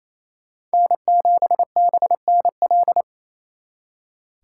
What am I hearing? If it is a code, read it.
Morse code, heard as a keyed tone, about 35 wpm, 700 Hz. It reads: N7BNL